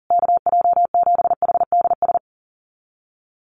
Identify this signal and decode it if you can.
Morse code, heard as a keyed tone, about 40 wpm, 700 Hz. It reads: KJ7HDS
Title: USA Call Signs Repeated 3x 40wpm